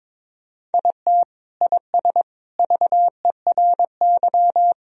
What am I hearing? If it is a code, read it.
Morse code, heard as a keyed tone, about 22 wpm, 700 Hz. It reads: IT IS VERY
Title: Sentences from Top 1000 Words - Review Part 1 - 22wpm